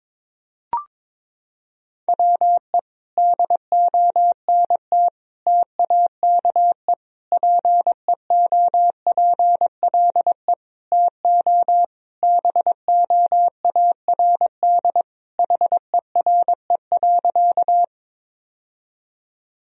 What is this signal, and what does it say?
Morse code, heard as a keyed tone, about 22 wpm, 700 Hz. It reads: WE DONT TAKE PEOPLE TO BOARD HERE.
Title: Sentences from Top 1000 Words - Review Part 4 - 22wpm